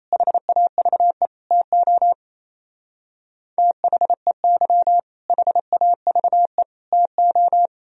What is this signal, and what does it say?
Morse code, heard as a keyed tone, about 32 wpm, 700 Hz. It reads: HAVE TO THEY HAVE TO
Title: Speed-Racing - Sets of 3 Words 28wpm